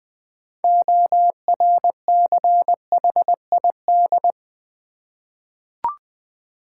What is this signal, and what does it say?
Morse code, heard as a keyed tone, about 20 wpm, 700 Hz. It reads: ORCHID
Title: Difficult Words with Hints - 20wpm